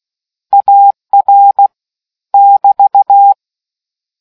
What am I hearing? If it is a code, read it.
Morse code, heard as a keyed tone, about 16 wpm, 800 Hz. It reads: AR <BT>